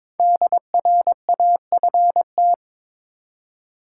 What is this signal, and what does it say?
Morse code, heard as a keyed tone, about 22 wpm, 700 Hz. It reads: DRAFT
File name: 5 Letter Words - First 3 Letters Given - 22wpm